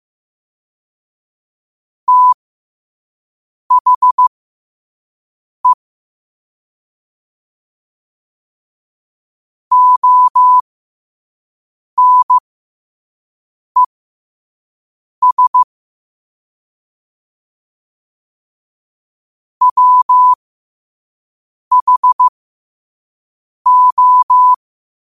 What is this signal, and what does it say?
Morse code, heard as a keyed tone, about 15 wpm, spaced as for 5 wpm, 1000 Hz. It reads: THE ONES WHO
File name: qotdfiles / 5wpm